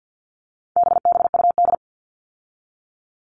Morse code, heard as a keyed tone, about 50 wpm, 700 Hz. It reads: BBUD
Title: All Cut Numbers - 4-Digit 50wpm